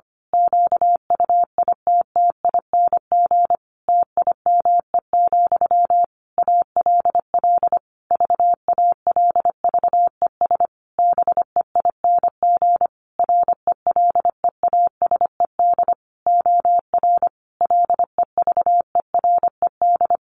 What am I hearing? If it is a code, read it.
Morse code, heard as a keyed tone, about 25 wpm, 700 Hz. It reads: QUITTING TIME, ALL VALVES BEING RELEASED OR LEVERED
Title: A Sound Of Thunder 25wpm